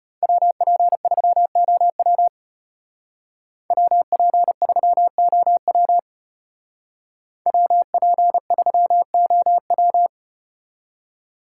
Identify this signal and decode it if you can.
Morse code, heard as a keyed tone, about 34 wpm, 700 Hz. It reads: WP3OW WP3OW WP3OW